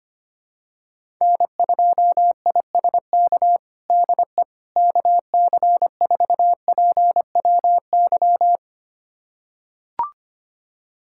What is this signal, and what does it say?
Morse code, heard as a keyed tone, about 25 wpm, 700 Hz. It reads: N2ISK DE KC4PWY
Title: USA Callsign DE Callsign 25wpm